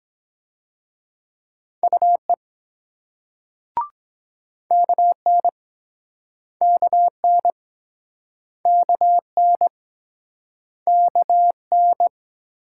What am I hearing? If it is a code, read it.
Morse code, heard as a keyed tone, about 23 wpm, 700 Hz. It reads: UE KN KN KN KN